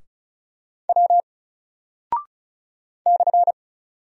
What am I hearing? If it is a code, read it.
Morse code, heard as a keyed tone, about 36 wpm, 700 Hz. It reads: W /
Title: Single Character (Letter:Number:Punctuation:Prosign) 35wpm